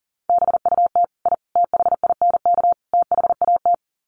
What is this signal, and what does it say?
Morse code, heard as a keyed tone, about 40 wpm, 700 Hz. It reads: BUT I THINK THAT
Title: Sets of 4 Words 40wpm